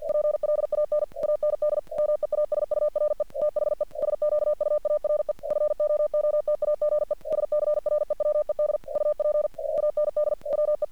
Morse code, heard as a keyed tone, about 48 wpm, 600 Hz. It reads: PLAN AND MEASURE THE SQUARE FOOTAGE SCREWED UP AND WE
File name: archived / CRSnet